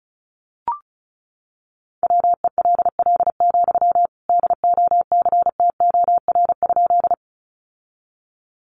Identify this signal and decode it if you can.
Morse code, heard as a keyed tone, about 34 wpm, 700 Hz. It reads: WELL, DOCTOR?
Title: Sentences from Top 1500 Words 35wpm